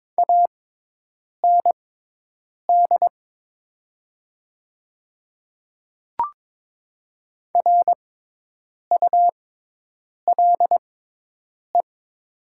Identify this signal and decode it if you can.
Morse code, heard as a keyed tone, about 22 wpm, spaced as for 8 wpm, 700 Hz. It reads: AND RULE
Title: 6x Character Spacing - Top 500 Words - 22wpm